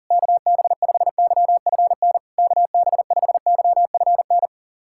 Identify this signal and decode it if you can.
Morse code, heard as a keyed tone, about 38 wpm, 690 Hz. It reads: KB5YFN KB5YFN
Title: SOTA QSO Practice 40wpm